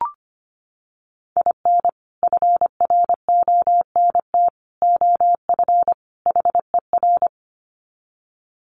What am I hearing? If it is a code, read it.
Morse code, heard as a keyed tone, about 25 wpm, 700 Hz. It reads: IN FRONT OF HER
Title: Sets of 4 Words 25wpm